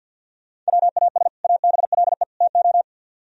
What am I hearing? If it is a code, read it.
Morse code, heard as a keyed tone, about 63 wpm, 700 Hz. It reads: WAS ABLE TO